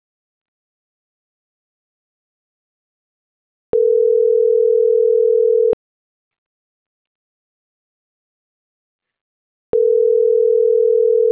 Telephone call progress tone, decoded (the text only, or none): ringback tone